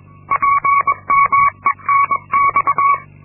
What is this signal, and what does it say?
Morse code, heard as a keyed tone, about 22 wpm, 1120 Hz. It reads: PMENX
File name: Linksmakalnis-2